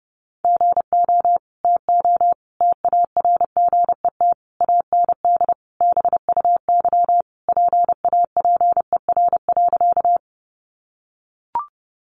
Morse code, heard as a keyed tone, about 30 wpm, 700 Hz. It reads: GO TO TARGET AND BUY PAPER.
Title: Sentences from Top 1200 Words 30wpm